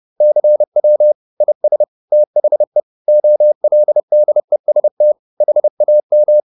Morse code, heard as a keyed tone, about 30 wpm, 600 Hz. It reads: CW IS THE OLDEST HAM